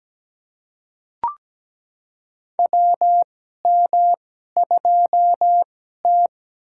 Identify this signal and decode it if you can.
Morse code, heard as a keyed tone, about 17 wpm, spaced as for 12 wpm, 700 Hz. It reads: WM2T
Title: USA Call Signs - 4 Characters - 2x Character Spacing - 17wpm